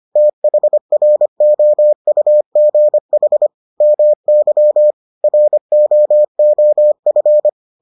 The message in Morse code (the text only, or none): THROUGH MY ROOF